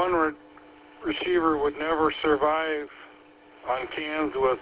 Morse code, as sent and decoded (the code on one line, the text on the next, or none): none